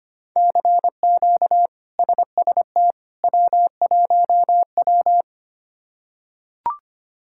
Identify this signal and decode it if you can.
Morse code, heard as a keyed tone, about 25 wpm, 700 Hz. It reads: CQ SST W1W
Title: K1USN SST QSO Practice - 25wpm